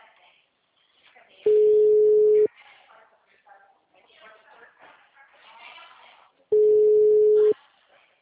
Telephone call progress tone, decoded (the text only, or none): ringback tone